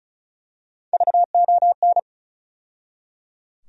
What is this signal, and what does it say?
Morse code, heard as a keyed tone, about 36 wpm, 700 Hz. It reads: VON
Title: USA Call Sign Postfixes - 3 Letters 35wpm